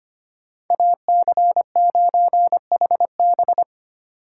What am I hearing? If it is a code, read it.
Morse code, heard as a keyed tone, about 31 wpm, 700 Hz. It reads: AC9HB
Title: Speed Racing - USA 2x2 Call Signs 25wpm